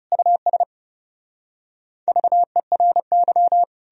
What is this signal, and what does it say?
Morse code, heard as a keyed tone, about 37 wpm, 700 Hz. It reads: US VERY
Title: Speed-Racing - Sets of 5 Words 30wpm